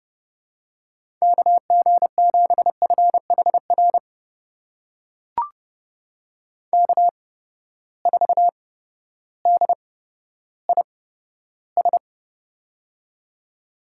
Morse code, heard as a keyed tone, about 30 wpm, spaced as for 14 wpm, 700 Hz. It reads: KG7FHR K4DIS
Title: USA Call Signs - 8x Character Spacing - 30wpm